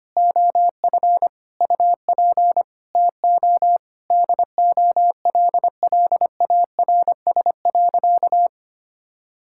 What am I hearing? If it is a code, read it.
Morse code, heard as a keyed tone, about 25 wpm, 700 Hz. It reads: OF UP TO DOLLARS.
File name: practice / 25wpm / Sentences with 9-Word Phrases - 25wpm